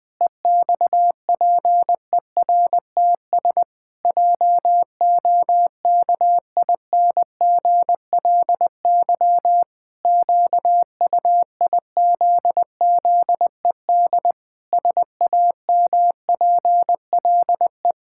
Morse code, heard as a keyed tone, about 20 wpm, 700 Hz. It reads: EXPERTS JOKINGLY QUIZZED SAMPLE
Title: Pangram Challenge v2 - 20wpm